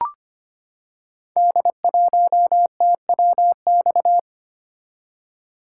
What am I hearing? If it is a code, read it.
Morse code, heard as a keyed tone, about 25 wpm, 700 Hz. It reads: D1TWX
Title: Code Groups - 5 Letter:Numbers - 25wpm